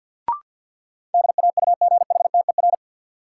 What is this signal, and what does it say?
Morse code, heard as a keyed tone, about 53 wpm, 700 Hz. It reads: DAUGHTER
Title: Sentences from Top 1500 Words - New Word Drill Alt - 50wpm